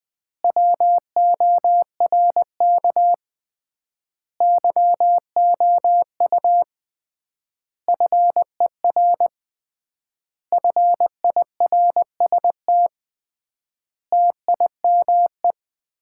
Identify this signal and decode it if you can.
Morse code, heard as a keyed tone, about 20 wpm, 700 Hz. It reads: WORK YOU FER FIRST TIME
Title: Common QSO Phrases - 3x Word-Spacing - 20wpm